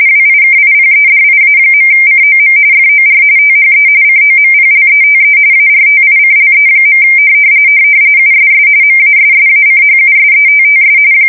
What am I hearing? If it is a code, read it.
RTTY, about 45.45 baud. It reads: WIKIPEDIA.COM FOR THE COMPLETE ARTICLE. USS MASSACHUSETTS
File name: fsk170 / USS-Massachusetts